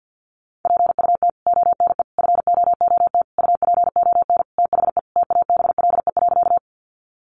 Morse code, heard as a keyed tone, about 52 wpm, 700 Hz. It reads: PUT ONE FOOT UPON THE TABLE.
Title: Sentences from Top 1100 Words 50wpm